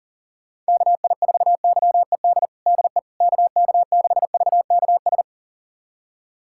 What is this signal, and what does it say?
Morse code, heard as a keyed tone, about 40 wpm, 700 Hz. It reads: KI4YED DE KK6VKS
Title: USA Callsign DE Callsign - 6 Characters - 40wpm